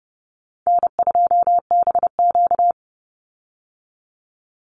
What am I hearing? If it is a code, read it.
Morse code, heard as a keyed tone, about 30 wpm, 700 Hz. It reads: N2BQ
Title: USA Call Signs 30wpm